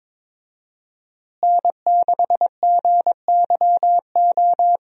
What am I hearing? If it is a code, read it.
Morse code, heard as a keyed tone, about 22 wpm, 700 Hz. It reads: N6GYO
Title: USA Call Signs - 5 Characters - Repeated 3x - 22wpm